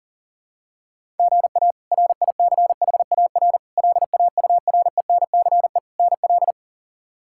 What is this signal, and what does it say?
Morse code, heard as a keyed tone, about 40 wpm, 700 Hz. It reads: GA RICHAR LAURENCE NL
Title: K1USN SST QSO Practice - 40wpm